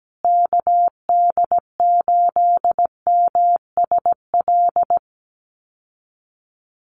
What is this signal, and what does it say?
Morse code, heard as a keyed tone, about 17 wpm, 700 Hz. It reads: KD8MSL
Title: USA Call Signs Repeated 3x 17wpm